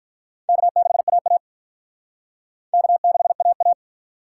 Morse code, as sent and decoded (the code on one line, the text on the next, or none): -.- -.... .- .- / -.- -.... .- .-
K6AA K6AA